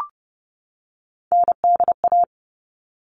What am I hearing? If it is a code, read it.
Morse code, heard as a keyed tone, about 30 wpm, 700 Hz. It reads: NDA
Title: All Cut Numbers - 3-Digit 30wpm